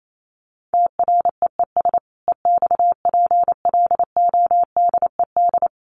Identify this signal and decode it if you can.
Morse code, heard as a keyed tone, about 28 wpm, 700 Hz. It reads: TREES EXPLODED
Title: A Sound Of Thunder 28wpm